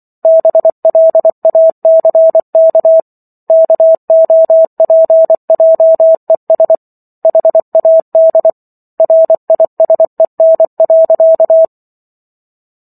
Morse code, heard as a keyed tone, about 24 wpm, 650 Hz. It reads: BLACK KOPJES HAD RISEN.